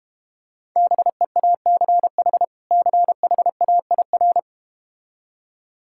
Morse code, heard as a keyed tone, about 32 wpm, 700 Hz. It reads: BEACH CHAIR